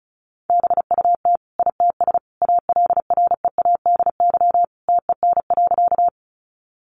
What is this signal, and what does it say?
Morse code, heard as a keyed tone, about 34 wpm, 700 Hz. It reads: BUT ITS ALREADY TEN.
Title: Sentences from Top 700 Words 35wpm